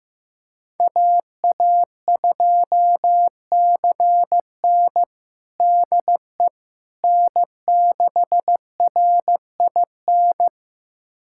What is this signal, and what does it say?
Morse code, heard as a keyed tone, about 16 wpm, 700 Hz. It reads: AA2CN DE N6RIN
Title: USA Callsign DE Callsign - 5 Characters - 15wpm